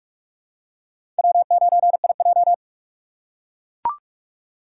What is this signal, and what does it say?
Morse code, heard as a keyed tone, about 42 wpm, 700 Hz. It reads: W9IJ
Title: USA Call Signs - 6x Character Spacing - 45wpm